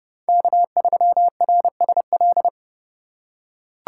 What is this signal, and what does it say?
Morse code, heard as a keyed tone, about 29 wpm, 700 Hz. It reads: K3RSL